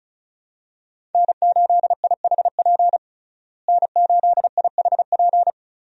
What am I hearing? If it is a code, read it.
Morse code, heard as a keyed tone, about 36 wpm, 700 Hz. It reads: N8IHP N8IHP
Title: USA Call Signs - 5 Characters - Repeated 3x - 35wpm